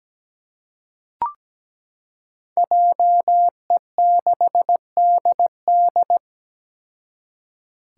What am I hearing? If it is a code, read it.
Morse code, heard as a keyed tone, about 17 wpm, 700 Hz. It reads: JE6DD
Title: Five Challenging Letter-Number Combinations 17wpm